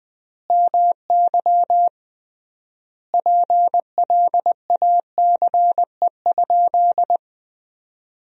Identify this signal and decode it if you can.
Morse code, heard as a keyed tone, about 20 wpm, 700 Hz. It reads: MY PLACE?
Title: 3x Word-Spacing - Sentences from Top 200 Words - 20wpm